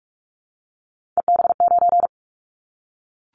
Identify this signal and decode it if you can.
Morse code, heard as a keyed tone, about 44 wpm, 700 Hz. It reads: EB9
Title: Callsign Prefixes for Top 100 DXCC - 45wpm